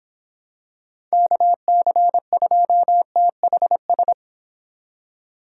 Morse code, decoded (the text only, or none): KC2THS